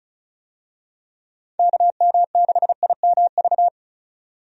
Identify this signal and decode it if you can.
Morse code, heard as a keyed tone, about 34 wpm, 700 Hz. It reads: KM6IMV
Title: USA Call Signs - 6 Characters - 35wpm